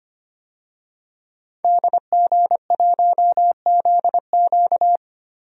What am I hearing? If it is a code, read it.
Morse code, heard as a keyed tone, about 25 wpm, 700 Hz. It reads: DG1ZQ
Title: DXCC-100 Call Signs - 5 Characters - Repeated 2x - 25wpm